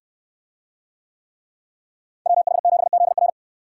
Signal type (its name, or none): Morse code